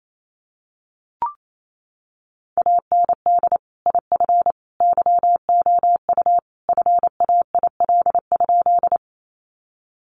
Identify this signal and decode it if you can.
Morse code, heard as a keyed tone, about 28 wpm, 700 Hz. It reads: AND IF YOU FAIL?
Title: Sentences from Top 1100 Words 28wpm